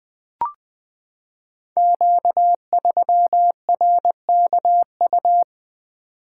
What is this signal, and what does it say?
Morse code, heard as a keyed tone, about 20 wpm, 700 Hz. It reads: Q3RKU
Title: Code Groups - 5 Letter:Numbers - 20wpm